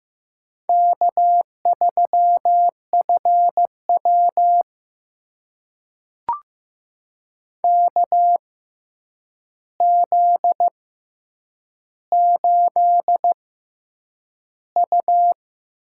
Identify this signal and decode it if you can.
Morse code, heard as a keyed tone, about 15 wpm, spaced as for 8 wpm, 700 Hz. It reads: K3FW KZ8U
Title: USA Call Signs - 4 Characters - 6x Character Spacing - 15wpm